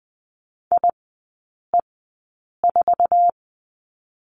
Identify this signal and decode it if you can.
Morse code, heard as a keyed tone, about 20 wpm, spaced as for 8 wpm, 700 Hz. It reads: IE4
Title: 047 Lesson 9e - Number 4 - Three Characters 20wpm